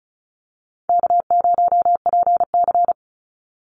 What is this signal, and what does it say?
Morse code, heard as a keyed tone, about 34 wpm, 700 Hz. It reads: K0PC
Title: Easy Call Signs Repeated 3x - 35wpm